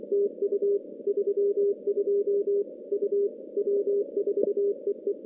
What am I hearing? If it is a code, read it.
Morse code, heard as a keyed tone, about 24 wpm, 400 Hz. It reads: TU 32 U W4EE
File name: contests / CW-SS-k5tr- -2200